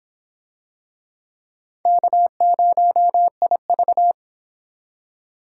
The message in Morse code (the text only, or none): K0IV